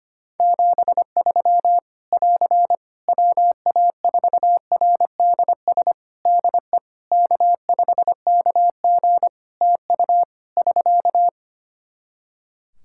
Morse code, heard as a keyed tone, about 25 wpm, 700 Hz. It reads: 73 <AR> WA4RDS DE K5KG TU <SK>
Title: Practice QSO - Exchange 4 - Ending - 25wpm